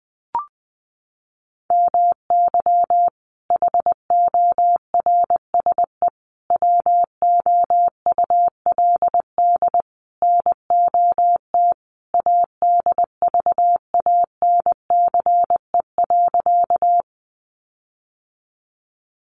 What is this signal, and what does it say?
Morse code, heard as a keyed tone, about 20 wpm, 700 Hz. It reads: MY HORSE WOULD NOT ADVANCE.
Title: Sentences from Top 1400 Words 20wpm